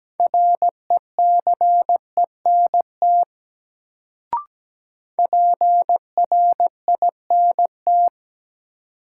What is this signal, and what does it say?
Morse code, heard as a keyed tone, about 17 wpm, 700 Hz. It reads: RECENT PRINT
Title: Sentences from Top 1400 Words - New Word Drill Alt - 17wpm